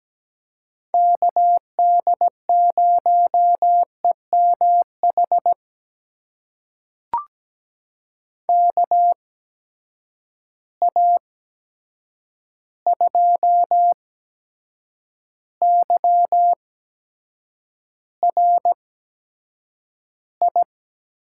Morse code, heard as a keyed tone, about 17 wpm, spaced as for 7 wpm, 700 Hz. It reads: KD0EMH KA2YRI
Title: USA Call Signs - 6 Characters - 8x Character Spacing - 17wpm